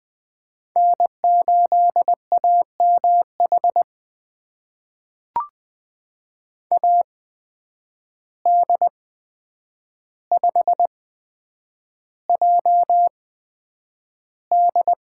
Morse code, heard as a keyed tone, about 20 wpm, spaced as for 9 wpm, 700 Hz. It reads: N8AMH AD5JD
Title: USA Call Signs - 5 Characters - 8x Character Spacing - 20wpm